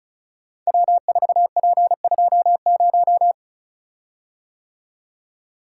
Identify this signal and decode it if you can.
Morse code, heard as a keyed tone, about 36 wpm, 690 Hz. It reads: W4P20
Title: Code Groups - 5 Letter:Numbers - 35wpm